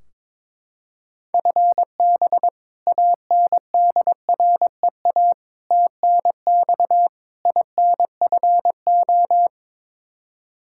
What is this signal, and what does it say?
Morse code, heard as a keyed tone, about 22 wpm, 700 Hz. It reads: FB ANDREA TNX INFO
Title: Practice QSO - Exchange 3 - More Info - 22wpm